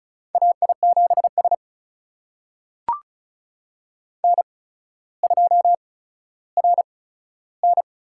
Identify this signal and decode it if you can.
Morse code, heard as a keyed tone, about 36 wpm, spaced as for 15 wpm, 700 Hz. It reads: AI7S N2RN
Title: USA Call Signs - 8x Character Spacing - 35wpm